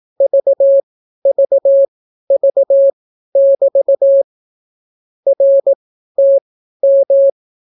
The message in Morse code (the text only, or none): VVV<BT> RTM